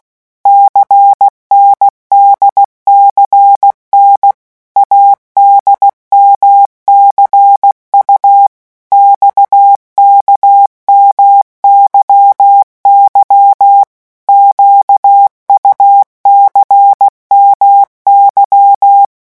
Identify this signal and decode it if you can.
Morse code, heard as a keyed tone, about 16 wpm, 800 Hz. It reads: CNDCN ADMCU XKMYY QUCMY
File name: morse mp3 / revision3-vitesse 15 mots